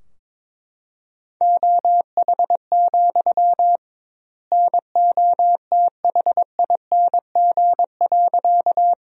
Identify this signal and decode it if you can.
Morse code, heard as a keyed tone, about 22 wpm, 700 Hz. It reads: OH, NOTHING.
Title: 2x Word-Spacing - Sentences from Top 500 Words - 22wpm